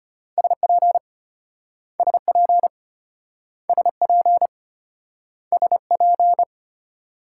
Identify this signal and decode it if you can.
Morse code, heard as a keyed tone, about 33 wpm, 700 Hz. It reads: SP SP SP SP